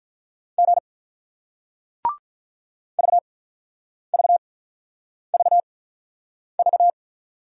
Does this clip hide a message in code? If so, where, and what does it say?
Yes: Morse code from 0.6 to 6.9 s, heard as a keyed tone, about 48 wpm, spaced as for 9 wpm, 700 Hz. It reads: G VVVV